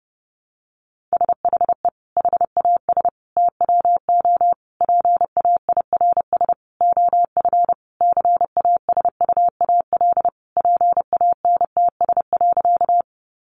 Morse code, heard as a keyed tone, about 30 wpm, 700 Hz. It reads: SHE HAS TWO PAIRS OF CASUAL PANTS.
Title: Additional Easy Phrases 30wpm